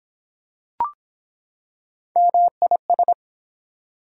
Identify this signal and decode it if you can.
Morse code, heard as a keyed tone, about 23 wpm, 700 Hz. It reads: MIS